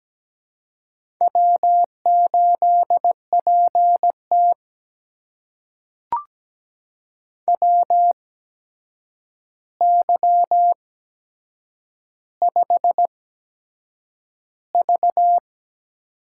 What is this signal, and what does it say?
Morse code, heard as a keyed tone, about 17 wpm, spaced as for 7 wpm, 700 Hz. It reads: W8PT WY5V
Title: USA Call Signs - 4 Characters - 8x Character Spacing - 17wpm